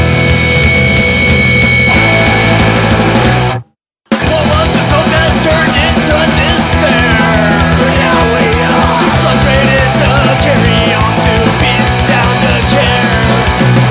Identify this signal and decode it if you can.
Dial-up modem handshake.